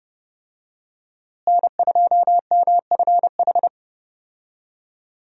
Morse code, heard as a keyed tone, about 32 wpm, 700 Hz. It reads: N2MFH
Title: USA Call Signs - 5 Characters - Repeated 2x - 30wpm